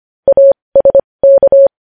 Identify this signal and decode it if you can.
Morse code, heard as a keyed tone, about 25 wpm, spaced as for 20 wpm, 550 Hz. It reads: ASK
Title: News Headlines at 20